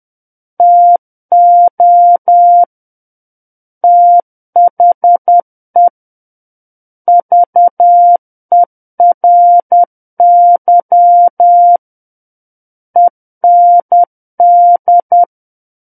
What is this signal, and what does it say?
Morse code, heard as a keyed tone, about 10 wpm, 700 Hz. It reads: TO THE VERY END
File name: Book One / 01-Chapter 1 - 10 WPM